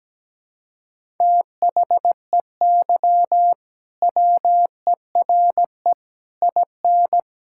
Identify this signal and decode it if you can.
Morse code, heard as a keyed tone, about 17 wpm, 700 Hz. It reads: THEY WERE IN
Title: Sentences from Top 1000 Words - Review Part 6 - 17wpm